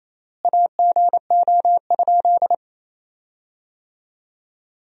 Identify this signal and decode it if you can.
Morse code, heard as a keyed tone, about 28 wpm, 700 Hz. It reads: AGO?
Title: 3x Word-Spacing - Sentences from Top 500 Words - 28wpm